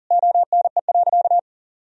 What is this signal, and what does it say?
Morse code, heard as a keyed tone, about 40 wpm, 700 Hz. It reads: ONE.
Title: Sentences from Top 1000 Words - Review Part 4 - 40wpm